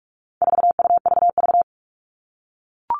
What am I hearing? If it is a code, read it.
Morse code, heard as a keyed tone, about 44 wpm, 700 Hz. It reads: 4UVV
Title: All Cut Numbers - 4-Digit 45wpm